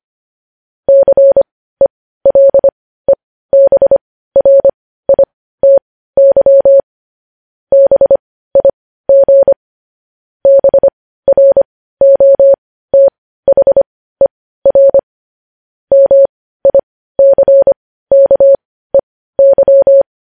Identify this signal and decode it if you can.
Morse code, heard as a keyed tone, about 25 wpm, spaced as for 15 wpm, 550 Hz. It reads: CELEBRITY BIG BROTHER MICKEY
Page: news-headlines-in-morse-code-at-15-wpm